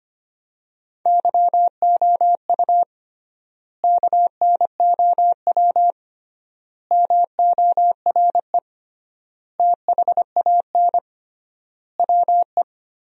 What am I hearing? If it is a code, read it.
Morse code, heard as a keyed tone, about 25 wpm, 700 Hz. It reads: YOU KNOW MORE THAN WE
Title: 3x Word-Spacing - Sentences from Top 100 Words - 25wpm